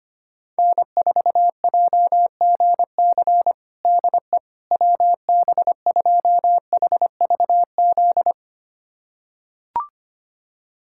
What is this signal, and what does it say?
Morse code, heard as a keyed tone, about 25 wpm, 700 Hz. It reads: N4JGC DE WB2HVZ
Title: USA Callsign DE Callsign 25wpm